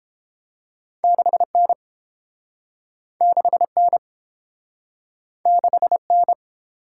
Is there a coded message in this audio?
Yes: Morse code from 1.0 to 6.3 s, heard as a keyed tone, about 30 wpm, 700 Hz. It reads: 6N 6N 6N